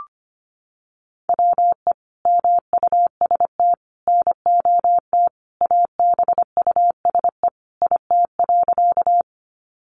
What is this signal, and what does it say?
Morse code, heard as a keyed tone, about 25 wpm, 700 Hz. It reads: WE MUST NOT ABUSE IT.
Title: Sentences from Top 1300 Words 25wpm